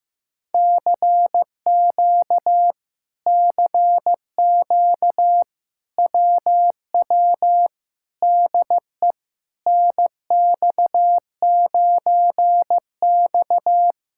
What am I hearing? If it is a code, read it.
Morse code, heard as a keyed tone, about 15 wpm, 700 Hz. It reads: CQ CQ WW DE NX9X
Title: Speed-Racing Common QSO Phrases Encore - 15wpm